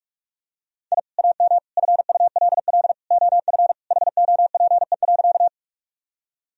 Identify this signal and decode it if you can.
Morse code, heard as a keyed tone, about 41 wpm, 700 Hz. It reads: I AM FULL OF HOPE.
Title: Sentences from Top 1000 Words - Review Part 3 - 45wpm